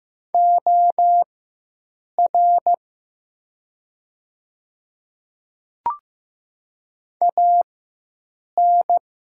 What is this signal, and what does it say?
Morse code, heard as a keyed tone, about 15 wpm, spaced as for 7 wpm, 700 Hz. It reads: OR AN
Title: 4x Character Spacing - Top 300 Words - 15wpm